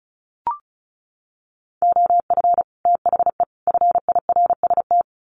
Morse code, heard as a keyed tone, about 34 wpm, 700 Hz. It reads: OF THE FIRST
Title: Sets of 3 Words 35wpm